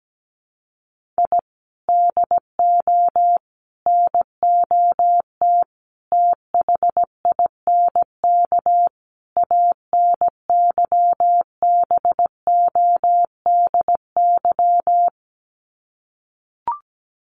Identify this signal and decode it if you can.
Morse code, heard as a keyed tone, about 17 wpm, 700 Hz. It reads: I DO NOT THINK ANYBODY
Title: Sets of 5 Words 17wpm